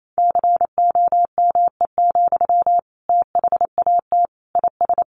Morse code, heard as a keyed tone, about 28 wpm, 700 Hz. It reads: COME, THAT IS
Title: Sentences from Top 700 Words 28wpm